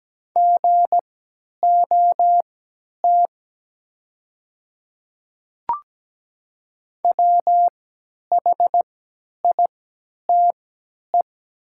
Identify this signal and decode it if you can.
Morse code, heard as a keyed tone, about 17 wpm, spaced as for 10 wpm, 690 Hz. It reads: GOT WHITE
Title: 3x Character Spacing - Top 300 Words - 17wpm